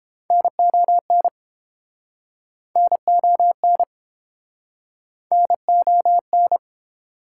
Morse code, heard as a keyed tone, about 33 wpm, 700 Hz. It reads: NON NON NON